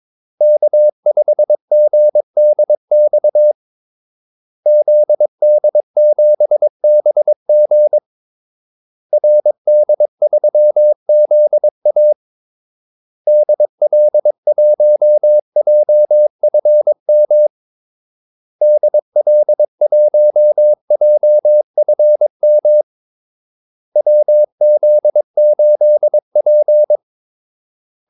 Morse code, heard as a keyed tone, about 22 wpm, 600 Hz. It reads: K5GDX ZD7BG RD3ZA DL1JFM DL1JFM WZ8P
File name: Callsigns-22wpm